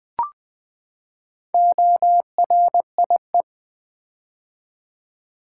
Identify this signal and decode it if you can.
Morse code, heard as a keyed tone, about 20 wpm, 700 Hz. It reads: ORIE
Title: Build A Word - 4 Letters At A Time - 11-Letter Words - 20wpm